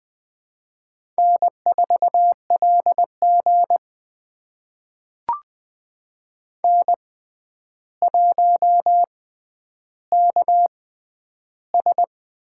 Morse code, heard as a keyed tone, about 21 wpm, spaced as for 10 wpm, 700 Hz. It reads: N4LG N1KS
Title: USA Call Signs - 4 Characters - 6x Character Spacing - 20wpm